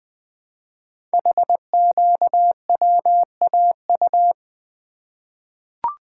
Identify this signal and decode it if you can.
Morse code, heard as a keyed tone, about 20 wpm, 700 Hz. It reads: HQWAU
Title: Code Groups - 5 Letters - 20wpm